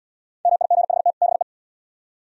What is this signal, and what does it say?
Morse code, heard as a keyed tone, about 68 wpm, 700 Hz. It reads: MIGHT BE